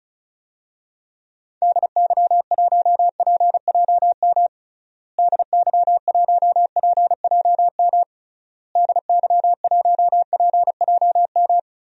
Morse code, heard as a keyed tone, about 36 wpm, 700 Hz. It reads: DY1PJM DY1PJM DY1PJM
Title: DXCC-100 Call Signs - 6 Characters - Repeated 3x - 35wpm